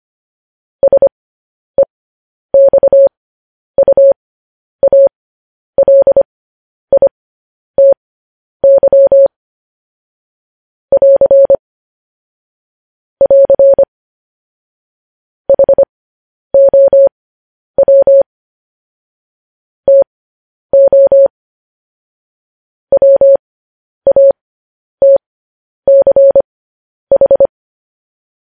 Morse code, heard as a keyed tone, about 25 wpm, spaced as for 10 wpm, 550 Hz. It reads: SEXUALITY <AR> <AR> HOW TO WATCH
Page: news-headlines-in-morse-code-at-10-wpm